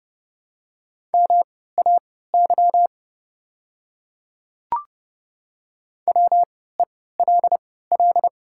Morse code, heard as a keyed tone, about 29 wpm, spaced as for 17 wpm, 690 Hz. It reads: MAY WELL
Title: 3x Character Spacing - Top 200 Words - 30wpm